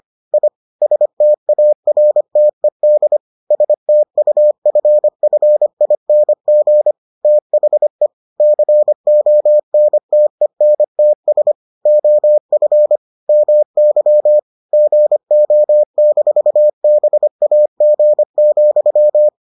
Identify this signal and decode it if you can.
Morse code, heard as a keyed tone, about 25 wpm, 600 Hz. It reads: I STARTED STUFFING THE CONTENTS OF MY GO-BAG,